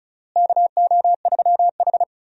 Morse code, heard as a keyed tone, about 36 wpm, 700 Hz. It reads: KO3H
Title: USA Call Signs - 4 Characters - Repeated 3x - 35wpm